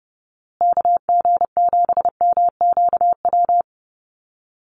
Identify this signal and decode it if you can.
Morse code, heard as a keyed tone, about 30 wpm, 700 Hz. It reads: KG7MQW
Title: USA Call Signs Repeated 2x 30wpm